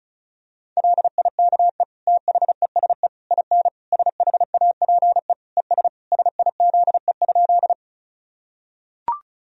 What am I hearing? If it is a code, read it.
Morse code, heard as a keyed tone, about 34 wpm, 700 Hz. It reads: LIKE THESE IN SHAPE ES SIZE?
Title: Sentences from Top 500 Words and Abbreviations 35wpm